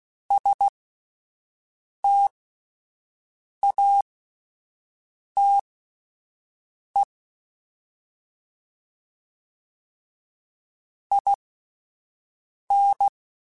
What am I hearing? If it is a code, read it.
Morse code, heard as a keyed tone, about 16 wpm, spaced as for 5 wpm, 800 Hz. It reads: STATE IN
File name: QSOs / QSOs-mp3s / WD0BGP 1@13